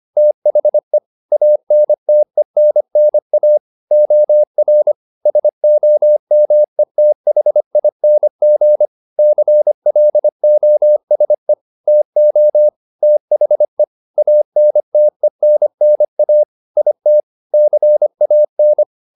Morse code, heard as a keyed tone, about 25 wpm, 600 Hz. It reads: THE ANTENNA OR SOMETHING CLOSE TO THE ANTENNA IT CAN